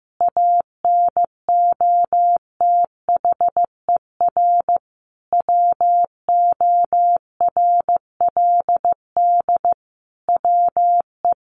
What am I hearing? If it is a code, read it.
Morse code, heard as a keyed tone, about 15 wpm, 700 Hz. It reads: ANOTHER WORLD WE
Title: Sentences from Top 1100 Words 15wpm